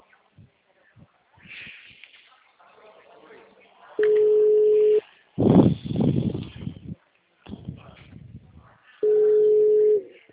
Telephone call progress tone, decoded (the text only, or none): ringback tone